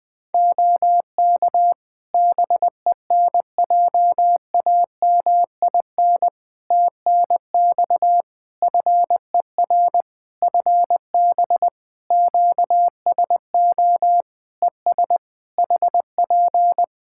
Morse code, heard as a keyed tone, about 21 wpm, 700 Hz. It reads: OK BENJAMIN TNX FER FB QSO ES HP
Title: Practice QSO - Exchange 4 - Ending - 20wpm